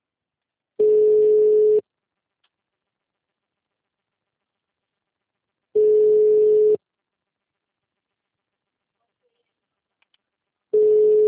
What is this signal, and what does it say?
Ringback tone (Europe, 425 Hz).